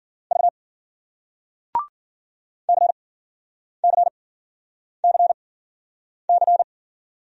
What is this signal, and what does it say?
Morse code, heard as a keyed tone, about 54 wpm, spaced as for 9 wpm, 700 Hz. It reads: V CCCC